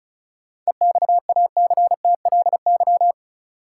Morse code, heard as a keyed tone, about 34 wpm, 700 Hz. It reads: EXACTLY
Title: Sentences from Top 900 Words - New Word Drill - 35wpm